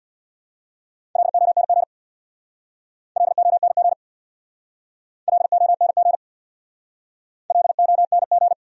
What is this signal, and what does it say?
Morse code, heard as a keyed tone, about 64 wpm, 700 Hz. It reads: LONG LONG LONG LONG